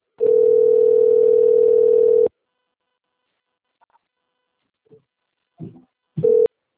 Ringback tone (North America), 440 + 480 Hz.